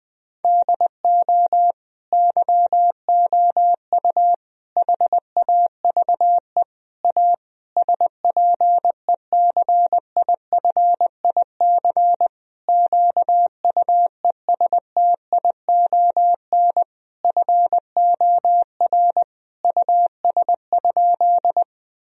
Morse code, heard as a keyed tone, about 20 wpm, 700 Hz. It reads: DO YOU HAVE A SPECIFIC QUESTION FOR US?
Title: Sentences from Top 1000 Words - Review Part 3 - 20wpm